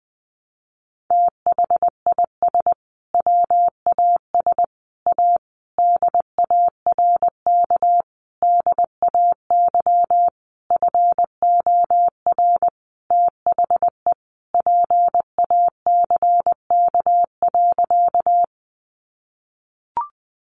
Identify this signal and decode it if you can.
Morse code, heard as a keyed tone, about 19 wpm, 700 Hz. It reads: THIS WAS A DARK DAY FOR THE PACK.
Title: Sentences from Top 1300 Words 20wpm